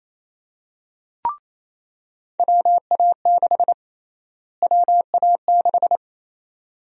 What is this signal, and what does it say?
Morse code, heard as a keyed tone, about 28 wpm, 700 Hz. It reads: WA6 WA6
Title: USA Call Sign Prefixes - Repeated 2x - 28wpm